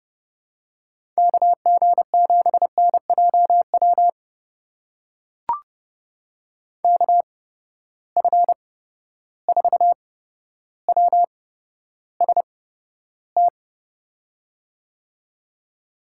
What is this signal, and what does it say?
Morse code, heard as a keyed tone, about 30 wpm, spaced as for 13 wpm, 700 Hz. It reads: KG7NJW KF4WST
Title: USA Call Signs - 6 Characters - 8x Character Spacing - 30wpm